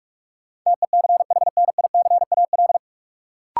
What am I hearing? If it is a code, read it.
Morse code, heard as a keyed tone, about 42 wpm, 700 Hz. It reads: TECHNICAL
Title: Sentences from Top 1100 Words - New Word Drill - 45wpm